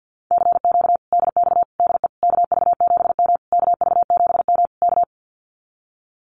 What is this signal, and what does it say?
Morse code, heard as a keyed tone, about 50 wpm, 700 Hz. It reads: CQ DX DE KV7M KV7M K
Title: Common QSO Phrases Encore 50wpm